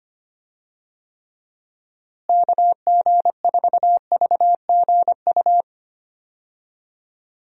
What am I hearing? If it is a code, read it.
Morse code, heard as a keyed tone, about 25 wpm, 700 Hz. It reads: KG4VGU
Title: USA Call Signs - 6 Characters - Repeated 2x - 25wpm